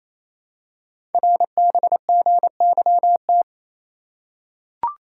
Morse code, heard as a keyed tone, about 27 wpm, 700 Hz. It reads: RBGYT
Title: Code Groups - 5 Letters - 28wpm